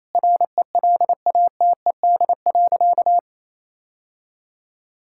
Morse code, heard as a keyed tone, about 27 wpm, 700 Hz. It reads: RELATED.
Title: Sentences from Top 1000 Words - Review Part 5 - 28wpm